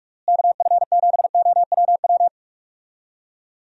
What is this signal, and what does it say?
Morse code, heard as a keyed tone, about 48 wpm, 700 Hz. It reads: KF7OWW
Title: USA Call Signs - 6 Characters - Repeated 3x - 45wpm